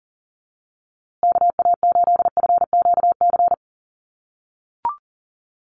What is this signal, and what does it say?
Morse code, heard as a keyed tone, about 40 wpm, 700 Hz. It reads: KA8FQC
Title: USA Call Signs Repeated 3x 40wpm